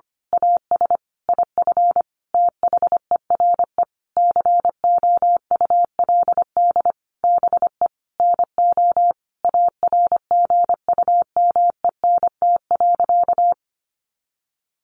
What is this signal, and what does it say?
Morse code, heard as a keyed tone, about 25 wpm, 700 Hz. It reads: AS IF THERE COULD BE NO ARGUMENT.
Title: A Sound Of Thunder 25wpm